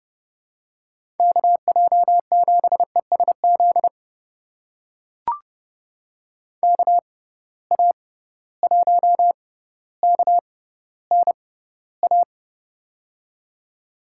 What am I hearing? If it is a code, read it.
Morse code, heard as a keyed tone, about 32 wpm, spaced as for 16 wpm, 700 Hz. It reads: KJ7ESZ KA1KNA
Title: USA Call Signs - 6 Characters - 6x Character Spacing - 30wpm